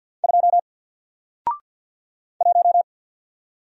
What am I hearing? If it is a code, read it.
Morse code, heard as a keyed tone, about 48 wpm, 700 Hz. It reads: 2 1
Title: Single Letter-Number Alt 50wpm